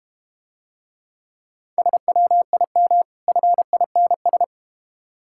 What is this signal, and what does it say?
Morse code, heard as a keyed tone, about 30 wpm, 690 Hz. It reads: SWIM FINS
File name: Easy Words - Speed-Racing with Hints - 25wpm